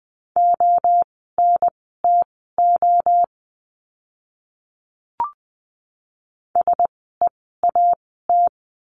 Morse code, heard as a keyed tone, about 20 wpm, spaced as for 15 wpm, 700 Hz. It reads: ONTO SEAT
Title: 036 Lesson 7f - Letter s - Words 20wpm